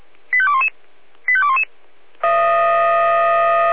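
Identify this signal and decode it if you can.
Five-tone selective call (ZVEI), heard as quick runs of five tones, tones of about 70 ms. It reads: 85211 85211
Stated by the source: ZVEI five-tone selective call received on an SDR.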